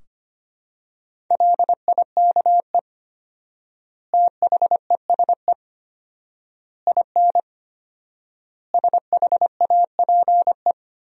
Morse code, heard as a keyed tone, about 25 wpm, 700 Hz. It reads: LIKE THESE IN SHAPE
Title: 4x Word-Spacing - Sentences from Top 500 Words - 25wpm